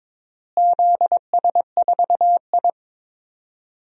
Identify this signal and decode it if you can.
Morse code, heard as a keyed tone, about 22 wpm, 700 Hz. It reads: ZS4I
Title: DXCC-100 Call Signs - 4 Characters - Repeated 3x - 22wpm